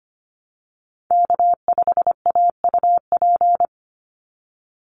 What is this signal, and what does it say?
Morse code, heard as a keyed tone, about 25 wpm, 700 Hz. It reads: K5AUP
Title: Call Signs Repeated 2x - 25wpm